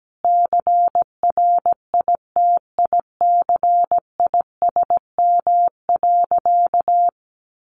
Morse code, heard as a keyed tone, about 17 wpm, 700 Hz. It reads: CRITICISM.
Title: Sentences from Top 1300 Words 17wpm